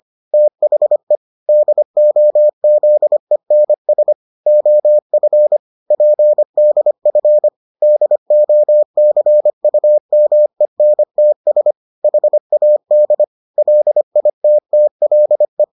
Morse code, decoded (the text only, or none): THE DOZENS OF PDF DOCUMENTS HAD LITTLE